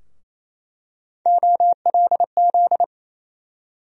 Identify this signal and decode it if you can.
Morse code, heard as a keyed tone, about 28 wpm, 700 Hz. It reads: OLZ
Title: USA Call Sign Postfixes - 3 Letters 28wpm